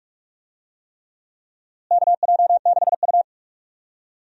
Morse code, heard as a keyed tone, about 46 wpm, 700 Hz. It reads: KJ6U
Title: USA Call Signs - 4 Characters - Repeated 3x - 45wpm